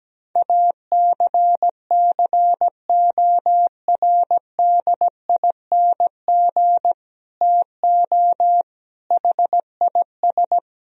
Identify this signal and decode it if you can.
Morse code, heard as a keyed tone, about 17 wpm, 700 Hz. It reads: ACCORDING TO HIS
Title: Sentences from Top 1000 Words - Review Part 4 - 17wpm